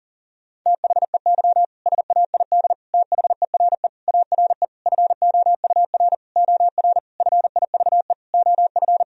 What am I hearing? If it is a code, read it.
Morse code, heard as a keyed tone, about 40 wpm, 700 Hz. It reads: THEY SAID THERE ARE FOUR OR FIVE OF
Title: Sentences from Top 1000 Words - Review Part 2 - 40wpm